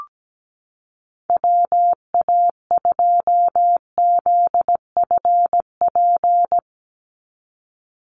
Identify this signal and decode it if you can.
Morse code, heard as a keyed tone, about 17 wpm, 700 Hz. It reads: WA2ZFP
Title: USA Call Signs 17wpm